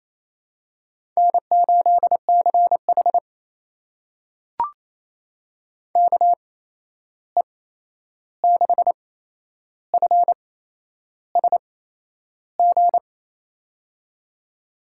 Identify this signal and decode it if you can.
Morse code, heard as a keyed tone, about 28 wpm, spaced as for 10 wpm, 700 Hz. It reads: N8CH KE6FSG
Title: USA Call Signs - 8x Character Spacing - 28wpm